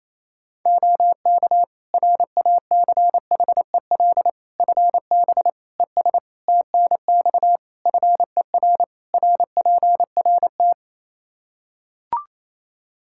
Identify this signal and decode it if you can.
Morse code, heard as a keyed tone, about 28 wpm, 700 Hz. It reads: OK RACHEL FB ES TNX FER RPRT
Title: Practice QSO - Exchange 2 - Info - 28wpm